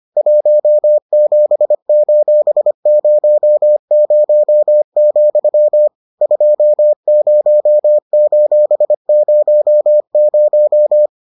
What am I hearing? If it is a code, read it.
Morse code, heard as a keyed tone, about 25 wpm, 600 Hz. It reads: 17:00, 20:00